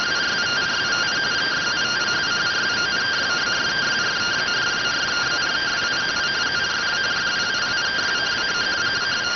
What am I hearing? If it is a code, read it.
Two-tone teleprinter signal, about 170 Hz shift.